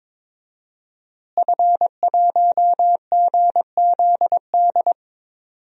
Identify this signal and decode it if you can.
Morse code, heard as a keyed tone, about 22 wpm, 700 Hz. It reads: F1GZD
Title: DXCC-100 Call Signs - 5 Characters - Repeated 2x - 22wpm